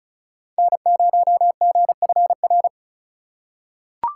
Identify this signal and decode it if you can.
Morse code, heard as a keyed tone, about 34 wpm, 700 Hz. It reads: N0GFR
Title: USA Call Signs - 5 Characters - 3x Character Spacing - 35wpm